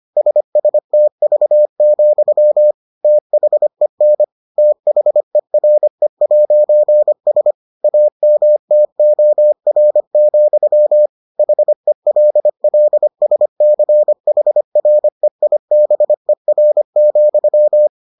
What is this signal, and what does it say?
Morse code, heard as a keyed tone, about 25 wpm, 600 Hz. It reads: SSTV, THEN THERE'S AMTOR, HELLSCHREIBER,